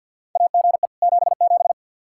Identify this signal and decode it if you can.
Morse code, heard as a keyed tone, about 52 wpm, 700 Hz. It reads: AGE 77
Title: Practice QSO - Exchange 3 - More Info - 50wpm